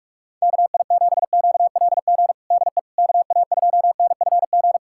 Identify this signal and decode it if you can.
Morse code, heard as a keyed tone, about 46 wpm, 700 Hz. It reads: KI7QLG DE KA2NFG
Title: USA Callsign DE Callsign - 6 Characters - 45wpm